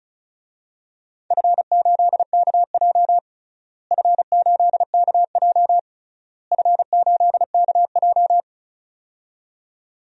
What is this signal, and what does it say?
Morse code, heard as a keyed tone, about 36 wpm, 700 Hz. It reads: F8KJ F8KJ F8KJ
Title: DXCC-100 Call Signs - 4 Characters - Repeated 3x - 35wpm